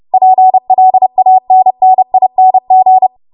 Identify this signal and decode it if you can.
Morse code, heard as a keyed tone, about 30 wpm, 750 Hz. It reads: PLANNING
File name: Morse / 30WPM